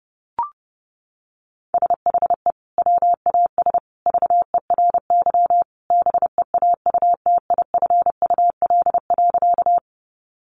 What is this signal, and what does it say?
Morse code, heard as a keyed tone, about 30 wpm, 700 Hz. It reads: SHE WAS VERY BEAUTIFUL.
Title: Sentences from Top 900 Words 30wpm